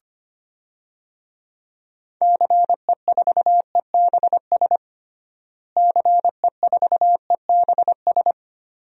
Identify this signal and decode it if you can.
Morse code, heard as a keyed tone, about 25 wpm, 700 Hz. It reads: CE4EBS CE4EBS
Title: DXCC-100 Call Signs - 6 Characters - Repeated 2x - 25wpm